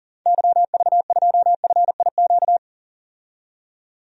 Morse code, heard as a keyed tone, about 40 wpm, 700 Hz. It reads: YV2FIQ
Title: DXCC-100 Call Signs - 6 Characters - 40wpm